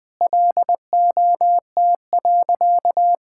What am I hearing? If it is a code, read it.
Morse code, heard as a keyed tone, about 20 wpm, 700 Hz. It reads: LOT.
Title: Sentences with 5-Word Phrases - 20wpm